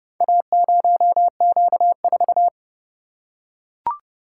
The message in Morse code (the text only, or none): A0Q4